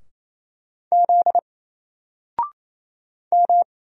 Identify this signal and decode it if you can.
Morse code, heard as a keyed tone, about 29 wpm, 700 Hz. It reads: Z M
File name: Single-Letter-Number-28wpm